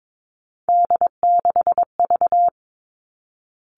Morse code, heard as a keyed tone, about 22 wpm, 700 Hz. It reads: D6V
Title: Three Challenging Letter-Number Combinations 22wpm